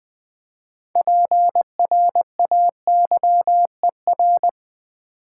Morse code, heard as a keyed tone, about 20 wpm, 700 Hz. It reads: PRAYER
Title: Build A Word - 2 Letters At A Time - 6-Letter Words - 20wpm